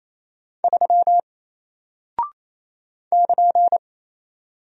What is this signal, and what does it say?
Morse code, heard as a keyed tone, about 27 wpm, 700 Hz. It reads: 3 <KN>
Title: Single Character (Letter:Number:Punctuation:Prosign) Alt 28wpm